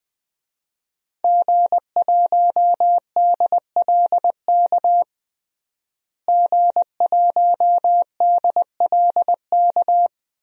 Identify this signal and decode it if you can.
Morse code, heard as a keyed tone, about 21 wpm, 700 Hz. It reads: G1DLK G1DLK
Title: DXCC-100 Call Signs - 5 Characters - Repeated 2x - 20wpm